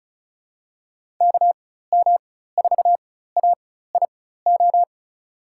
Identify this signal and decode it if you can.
Morse code, heard as a keyed tone, about 34 wpm, spaced as for 16 wpm, 700 Hz. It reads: KM4AIO
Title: USA Call Signs - 4x Character Spacing - 35wpm